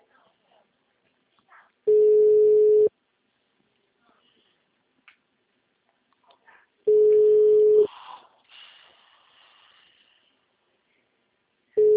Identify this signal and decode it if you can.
Ringback tone (Europe, 425 Hz), with speech.